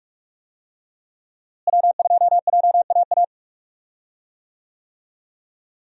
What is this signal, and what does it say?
Morse code, heard as a keyed tone, about 44 wpm, 700 Hz. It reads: W2JAA
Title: USA Call Signs - 5 Characters - Repeated 2x - 45wpm